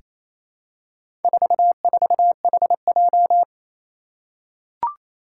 Morse code, heard as a keyed tone, about 27 wpm, 700 Hz. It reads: 44HJ
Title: Four Challenging Letter-Number Combinations 28wpm